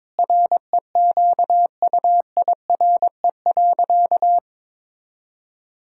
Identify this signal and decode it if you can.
Morse code, heard as a keyed tone, about 22 wpm, 700 Hz. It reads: REQUIRE.
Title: Sentences from Top 1000 Words - Review Part 6 - 22wpm